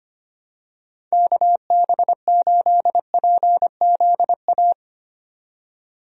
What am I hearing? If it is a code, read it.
Morse code, heard as a keyed tone, about 25 wpm, 700 Hz. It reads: KB8PZA
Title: USA Call Signs - 6 Characters - 25wpm